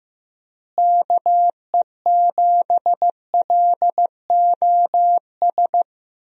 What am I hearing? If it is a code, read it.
Morse code, heard as a keyed tone, about 15 wpm, 700 Hz. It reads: KE7LOS
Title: USA Call Signs - 6 Characters - Repeated 3x - 15wpm